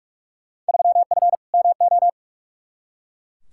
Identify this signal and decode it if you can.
Morse code, heard as a keyed tone, about 42 wpm, 700 Hz. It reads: 3F MO
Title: Field Day Exchanges Encore - 45wpm